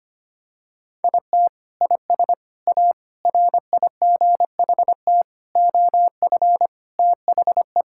Morse code, heard as a keyed tone, about 25 wpm, 700 Hz. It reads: IT IS A RIGHT OF THE
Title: Sentences from Top 1000 Words - Review Part 5 - 25wpm